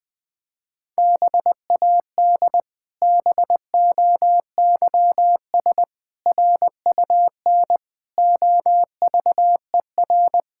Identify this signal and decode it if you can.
Morse code, heard as a keyed tone, about 20 wpm, 700 Hz. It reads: BAD BOYS RUN OVER
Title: Pangram Challenge v2 - 20wpm